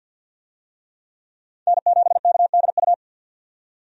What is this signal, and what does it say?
Morse code, heard as a keyed tone, about 52 wpm, 700 Hz. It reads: N7KDU
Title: USA Call Signs - 5 Characters - 50wpm